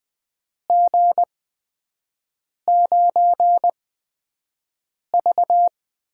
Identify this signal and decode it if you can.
Morse code, heard as a keyed tone, about 20 wpm, spaced as for 5 wpm, 700 Hz. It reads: G9V
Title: USA Call Signs - 4 Characters - 8x Character Spacing - 20wpm